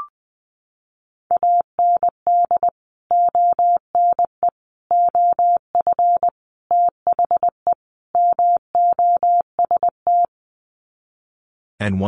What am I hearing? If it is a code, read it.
Morse code, heard as a keyed tone, about 19 wpm, 700 Hz. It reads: AND ONE OF THE MOST
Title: Sets of 5 Words Encore 20wpm